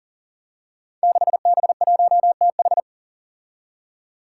Morse code, heard as a keyed tone, about 42 wpm, 700 Hz. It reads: 6B1TH
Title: Five Challenging Letter-Number Combinations 40wpm